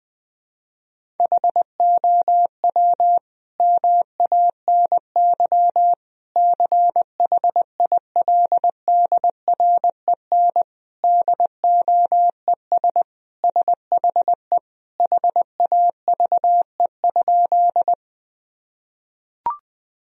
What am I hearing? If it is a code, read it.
Morse code, heard as a keyed tone, about 21 wpm, 700 Hz. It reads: HOW MANY CHILDREN DOES SHE HAVE?
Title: Sentences from Top 1000 Words - Review Part 1 - 20wpm